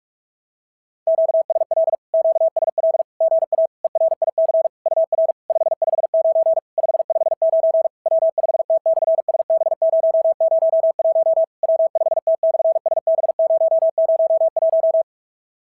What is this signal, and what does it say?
Morse code, heard as a keyed tone, about 46 wpm, 660 Hz. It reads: QSL QSL GA ERIC UR 559 559 W5T/SB001 W5T/SB001
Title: SOTA QSO Practice 45wpm